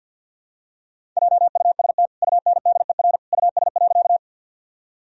Morse code, heard as a keyed tone, about 47 wpm, 700 Hz. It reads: JUST UNDER US.
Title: Sentences from Top 1000 Words - Review Part 1 - 50wpm